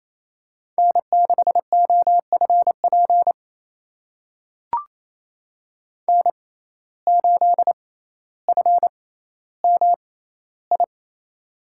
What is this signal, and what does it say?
Morse code, heard as a keyed tone, about 28 wpm, spaced as for 14 wpm, 700 Hz. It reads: N6OFP N8FMI
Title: USA Call Signs - 5 Characters - 6x Character Spacing - 28wpm